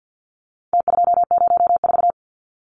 Morse code, heard as a keyed tone, about 50 wpm, 700 Hz. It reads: T204
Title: Numbers - Contest Style - 4-Digit 50wpm